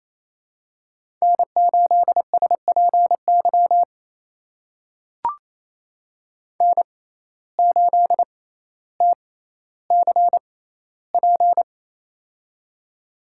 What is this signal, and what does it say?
Morse code, heard as a keyed tone, about 28 wpm, spaced as for 14 wpm, 700 Hz. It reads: N8SPY N8TCP
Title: USA Call Signs - 5 Characters - 6x Character Spacing - 28wpm